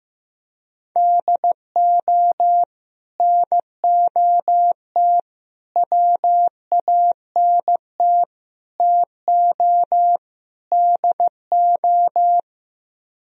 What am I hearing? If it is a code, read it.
Morse code, heard as a keyed tone, about 15 wpm, 700 Hz. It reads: DO NOT WANT TO DO
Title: Speed-Racing - Sets of 5 Words 15wpm